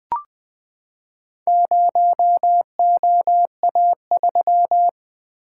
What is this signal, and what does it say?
Morse code, heard as a keyed tone, about 20 wpm, 700 Hz. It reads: 0OA3
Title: Four Letter-Number Combinations - 20wpm